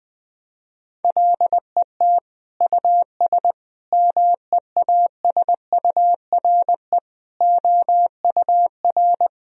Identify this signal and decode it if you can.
Morse code, heard as a keyed tone, about 21 wpm, 700 Hz. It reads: LET US MEASURE OUR
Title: Sentences from Top 1000 Words - Review Part 2 - 20wpm